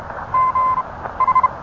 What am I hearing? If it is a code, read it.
Morse code, heard as a keyed tone, about 31 wpm, 1030 Hz. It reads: G H